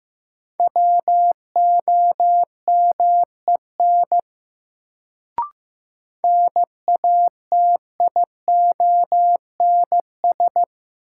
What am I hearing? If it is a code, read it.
Morse code, heard as a keyed tone, about 15 wpm, 700 Hz. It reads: WOMEN NATIONS
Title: Sets of 3 Words - New Word Drill - 15wpm